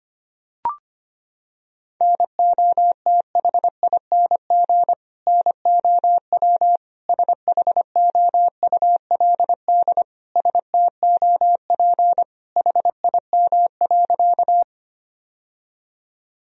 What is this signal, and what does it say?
Morse code, heard as a keyed tone, about 25 wpm, 700 Hz. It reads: NOTHING NOW SHOULD STOP HIM.
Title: Sentences from Top 1000 Words - Review Part 1 - 25wpm